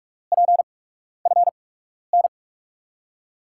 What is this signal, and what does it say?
Morse code, heard as a keyed tone, about 42 wpm, spaced as for 12 wpm, 700 Hz. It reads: PFN
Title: USA Call Signs - 6 Characters - 8x Character Spacing - 45wpm